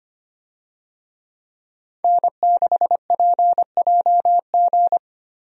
Morse code, heard as a keyed tone, about 25 wpm, 700 Hz. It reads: N6PJG
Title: USA Call Signs - 5 Characters - 25wpm